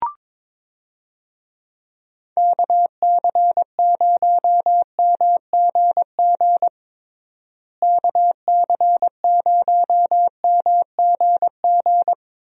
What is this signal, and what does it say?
Morse code, heard as a keyed tone, about 21 wpm, 700 Hz. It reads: KC0MGG KC0MGG
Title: USA Call Signs - 6 Characters - Repeated 2x - 22wpm